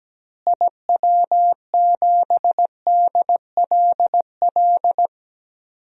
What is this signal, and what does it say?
Morse code, heard as a keyed tone, about 17 wpm, 700 Hz. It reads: IW7DLL
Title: DXCC-100 Call Signs - 6 Characters - Repeated 3x - 17wpm